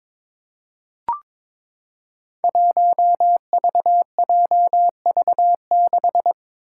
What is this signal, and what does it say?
Morse code, heard as a keyed tone, about 22 wpm, 700 Hz. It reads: 1VJV6
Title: Five Challenging Letter-Number Combinations 22wpm